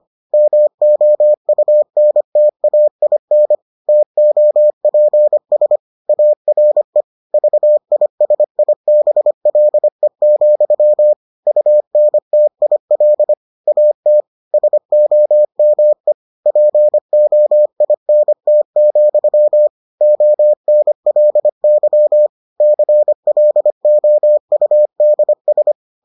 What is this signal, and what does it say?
Morse code, heard as a keyed tone, about 25 wpm, 600 Hz. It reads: MOUNTAIN TOPS ARE VISIBLE, UNTIL AT SOME POINT, ONLY CLOUDS